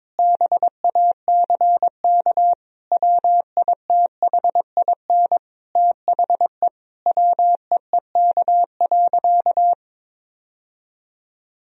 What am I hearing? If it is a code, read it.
Morse code, heard as a keyed tone, about 22 wpm, 700 Hz. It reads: BACK WITHIN THE WEEK.
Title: Sentences from Top 1000 Words - Review Part 4 - 22wpm